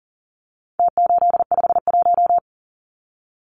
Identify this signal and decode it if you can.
Morse code, heard as a keyed tone, about 40 wpm, 700 Hz. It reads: T851
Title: Numbers - Contest Style - 4-Digit 40wpm